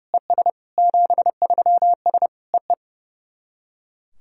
Morse code, heard as a keyed tone, about 30 wpm, 700 Hz. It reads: ES 73S EE
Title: SOTA QSO Practice 30wpm